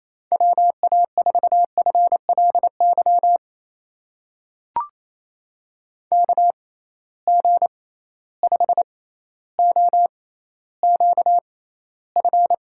Morse code, heard as a keyed tone, about 28 wpm, spaced as for 14 wpm, 700 Hz. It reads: WA4FLY KG5OQF
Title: USA Call Signs - 6 Characters - 6x Character Spacing - 28wpm